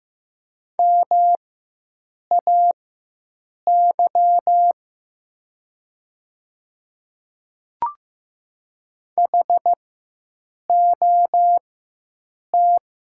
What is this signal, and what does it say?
Morse code, heard as a keyed tone, about 15 wpm, spaced as for 7 wpm, 700 Hz. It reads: MAY HOT
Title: 4x Character Spacing - Top 100 Words - 15wpm